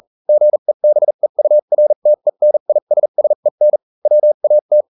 Morse code, heard as a keyed tone, about 40 wpm, spaced as for 35 wpm, 600 Hz. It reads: GEBEURTENISSEN WAT